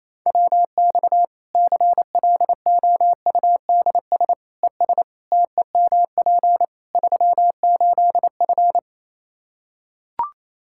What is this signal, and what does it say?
Morse code, heard as a keyed tone, about 28 wpm, 700 Hz. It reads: WX CLOUDS ES TEMP 38F
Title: Practice QSO - Exchange 2 - Info - 28wpm